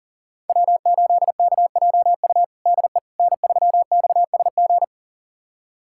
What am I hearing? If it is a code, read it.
Morse code, heard as a keyed tone, about 40 wpm, 700 Hz. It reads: W8KJU DE N3XSG
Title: USA Callsign DE Callsign - 5 Characters - 40wpm